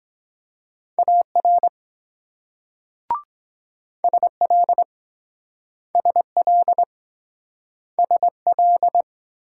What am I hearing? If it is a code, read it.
Morse code, heard as a keyed tone, about 23 wpm, 700 Hz. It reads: AR SL SL SL